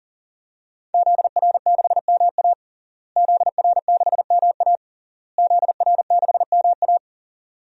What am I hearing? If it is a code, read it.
Morse code, heard as a keyed tone, about 40 wpm, 700 Hz. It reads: ZR6MA ZR6MA ZR6MA
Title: DXCC-100 Call Signs - 5 Characters - Repeated 3x - 40wpm